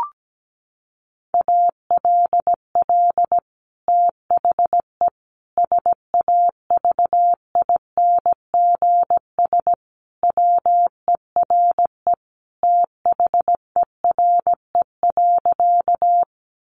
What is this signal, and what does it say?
Morse code, heard as a keyed tone, about 17 wpm, 700 Hz. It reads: ALL THE SAVINGS WERE THERE.
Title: Sentences from Top 1200 Words 17wpm